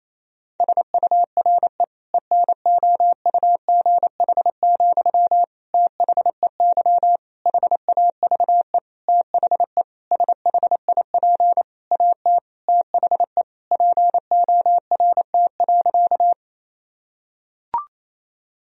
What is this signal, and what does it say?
Morse code, heard as a keyed tone, about 28 wpm, 700 Hz. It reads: SURE ENOUGH, THEY HAVE THE SHIP AT THE PORT.
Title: Sentences from Top 1000 Words - Review Part 2 - 28wpm